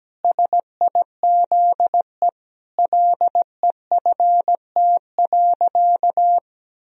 Morse code, heard as a keyed tone, about 17 wpm, 700 Hz. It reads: SIZE LEFT.
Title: Sentences from Top 1000 Words - Review Part 2 - 17wpm